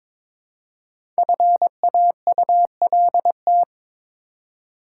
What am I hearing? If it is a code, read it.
Morse code, heard as a keyed tone, about 22 wpm, 700 Hz. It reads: FAULT
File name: 5 Letter Words - First 3 Letters Given - 22wpm